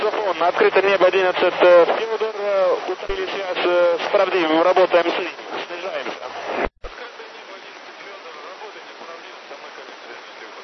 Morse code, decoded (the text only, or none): none